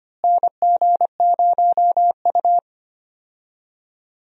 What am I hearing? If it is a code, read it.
Morse code, heard as a keyed tone, about 25 wpm, 700 Hz. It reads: NG0U
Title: USA Call Signs - 4 Characters - Repeated 3x - 25wpm